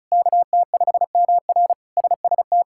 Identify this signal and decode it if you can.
Morse code, heard as a keyed tone, about 36 wpm, 700 Hz. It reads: KT5MR SST
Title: K1USN SST QSO Practice - 35wpm